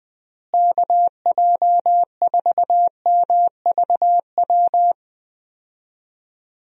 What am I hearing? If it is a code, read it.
Morse code, heard as a keyed tone, about 21 wpm, 700 Hz. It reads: KJ4MVW
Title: USA Call Signs - 6 Characters - Repeated 3x - 20wpm